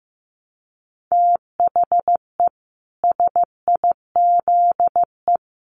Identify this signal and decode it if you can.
Morse code, heard as a keyed tone, about 15 wpm, 700 Hz. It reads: THE SIZE
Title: Sets of 4 Words 15wpm